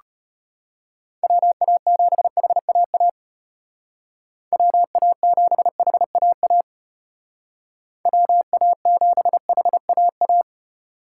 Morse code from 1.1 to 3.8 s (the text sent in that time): WA7HAA